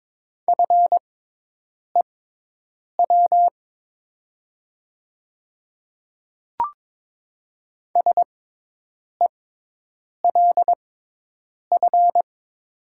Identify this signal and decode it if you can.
Morse code, heard as a keyed tone, about 22 wpm, spaced as for 8 wpm, 700 Hz. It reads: FEW SELF
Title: 6x Character Spacing - Top 500 Words - 22wpm